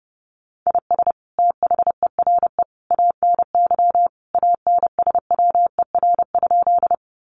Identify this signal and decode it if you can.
Morse code, heard as a keyed tone, about 30 wpm, 700 Hz. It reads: IS THERE ANY ANSWER?
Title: Sentences from Top 300 Words 30wpm